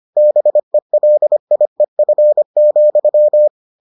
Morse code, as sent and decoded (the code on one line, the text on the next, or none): -... .-.. .. ..-. --..--
BELIEF,